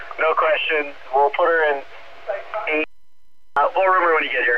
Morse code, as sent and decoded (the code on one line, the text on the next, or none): none